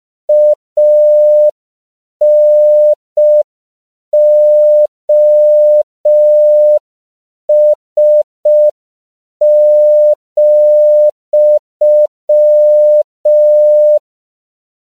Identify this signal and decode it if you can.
Morse code, heard as a keyed tone, about 5 wpm, 600 Hz. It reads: ANOS,
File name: CW / 5PPM